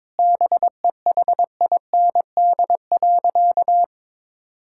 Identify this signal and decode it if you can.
Morse code, heard as a keyed tone, about 21 wpm, 700 Hz. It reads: BEHIND.
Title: Sentences from Top 1600 Words 22wpm